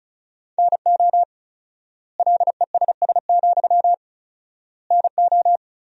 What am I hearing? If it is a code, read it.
Morse code, heard as a keyed tone, about 34 wpm, 700 Hz. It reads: NO LESS, NO
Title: 4x Word-Spacing - Sentences from Top 400 Words - 35wpm